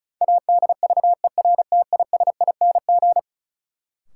Morse code, heard as a keyed tone, about 36 wpm, 700 Hz. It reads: ADVERTISING
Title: Sentences from Top 1000 Words - Review Part 6 - New Word Drill - 35wpm